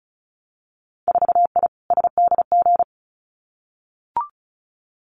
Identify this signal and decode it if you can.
Morse code, heard as a keyed tone, about 36 wpm, 700 Hz. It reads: 4I SDG
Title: Winter Field Day Exchanges Encore 35wpm